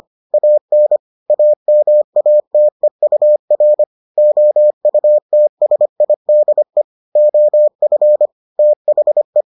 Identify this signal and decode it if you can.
Morse code, heard as a keyed tone, about 25 wpm, 600 Hz. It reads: AN AMATEUR OUTSIDE OF THE